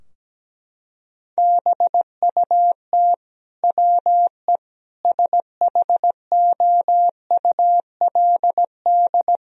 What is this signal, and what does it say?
Morse code, heard as a keyed tone, about 17 wpm, 700 Hz. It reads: BUT WE SHOULD
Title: Sentences from Top 1000 Words - Review Part 4 - 17wpm